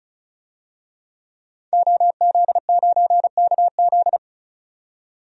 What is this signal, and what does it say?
Morse code, heard as a keyed tone, about 35 wpm, 700 Hz. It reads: OZ9KZ
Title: DXCC-100 Call Signs - 5 Characters - 35wpm